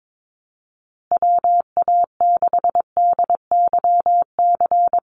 Morse code, heard as a keyed tone, about 22 wpm, 700 Hz. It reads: WA6DYC
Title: USA Call Signs Repeated 3x 22wpm